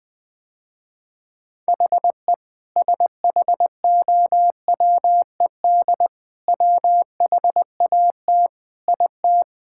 Morse code, heard as a keyed tone, about 20 wpm, 700 Hz. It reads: HE SHOWED WHAT IT
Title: Sentences with 5-Word Phrases - Spoken Beginning - 20wpm